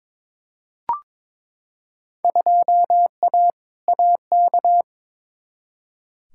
Morse code, heard as a keyed tone, about 22 wpm, 700 Hz. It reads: 2A AK
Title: Field Day Exchanges Encore - 22wpm